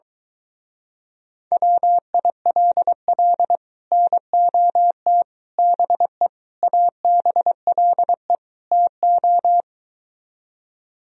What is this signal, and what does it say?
Morse code, heard as a keyed tone, about 23 wpm, 700 Hz. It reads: WILL NOT BE ABLE TO